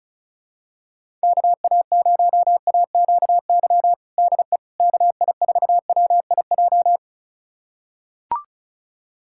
Morse code, heard as a keyed tone, about 34 wpm, 700 Hz. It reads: KA0AQY DE KI4WIJ
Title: USA Callsign DE Callsign - 6 Characters - 35wpm